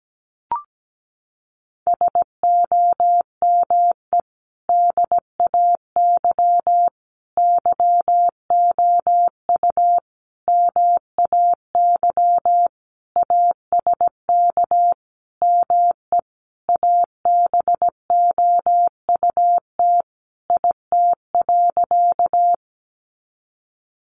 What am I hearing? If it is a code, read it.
Morse code, heard as a keyed tone, about 17 wpm, 700 Hz. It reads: SOME DAY YOU MAY ASK ME ABOUT IT.
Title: Sentences from Top 200 Words 17wpm